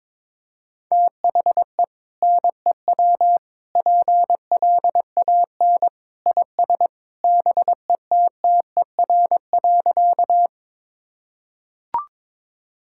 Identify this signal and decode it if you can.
Morse code, heard as a keyed tone, about 22 wpm, 700 Hz. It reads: THE NEW PLAN IS BETTER.
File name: practice / 22wpm / Sentences with 2-Word Phrases - 22wpm